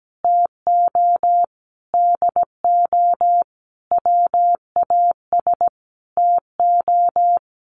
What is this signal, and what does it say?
Morse code, heard as a keyed tone, about 17 wpm, 700 Hz. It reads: TO DO WAS TO
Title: Sentences from Top 1100 Words 17wpm